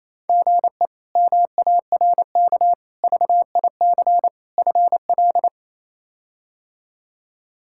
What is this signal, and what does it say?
Morse code, heard as a keyed tone, about 27 wpm, 700 Hz. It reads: GE MARK VIC FL
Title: K1USN SST QSO Practice - 28wpm